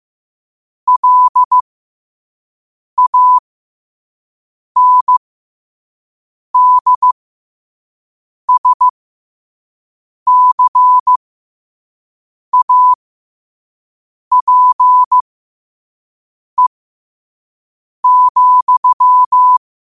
Morse code, heard as a keyed tone, about 15 wpm, spaced as for 5 wpm, 1000 Hz. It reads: LANDSCAPE,
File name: qotdfiles / 5wpm